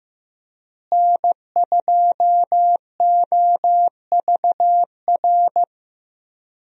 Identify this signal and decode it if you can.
Morse code, heard as a keyed tone, about 15 wpm, 700 Hz. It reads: N2OVR
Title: USA Call Signs - 5 Characters - Repeated 3x - 15wpm